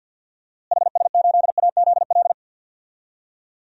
Morse code, heard as a keyed tone, about 51 wpm, 700 Hz. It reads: HS8AZL
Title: DXCC-100 Call Signs - 6 Characters - Repeated 2x - 50wpm